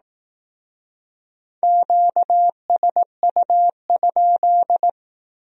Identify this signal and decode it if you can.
Morse code, heard as a keyed tone, about 18 wpm, 700 Hz. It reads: QSU?
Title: Speed-Racing Q-Codes - 15wpm